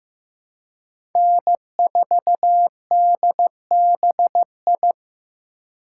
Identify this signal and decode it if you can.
Morse code, heard as a keyed tone, about 18 wpm, 700 Hz. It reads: N4DBI
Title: Speed Racing - USA 1x3 Call Signs 15wpm